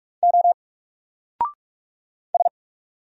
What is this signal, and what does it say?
Morse code, heard as a keyed tone, about 46 wpm, 700 Hz. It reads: O S
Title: Single Letter-Number 45wpm